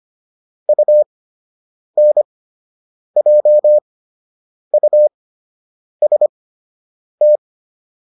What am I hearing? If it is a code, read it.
Morse code, heard as a keyed tone, about 25 wpm, spaced as for 8 wpm, 600 Hz. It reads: UNJUST